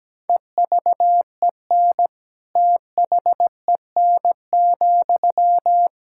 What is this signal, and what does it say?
Morse code, heard as a keyed tone, about 17 wpm, 700 Hz. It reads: EVEN THEN,
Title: Sentences from Top 1600 Words 17wpm